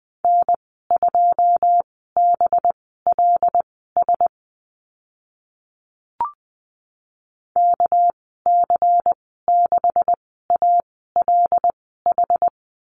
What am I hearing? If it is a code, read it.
Morse code, heard as a keyed tone, about 20 wpm, spaced as for 14 wpm, 700 Hz. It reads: N2BLS KC6ALH
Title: 219 Lesson 35g - Number 0 - Call Signs 20wpm